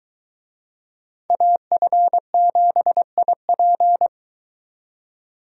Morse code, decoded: AF7IP

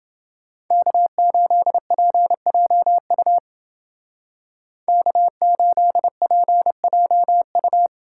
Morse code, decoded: K8PJU K8PJU

